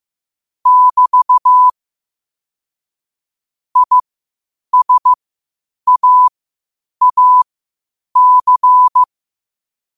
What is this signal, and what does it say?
Morse code, heard as a keyed tone, about 15 wpm, spaced as for 8 wpm, 1000 Hz. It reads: <BT> ISAAC
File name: qotdfiles / 8wpm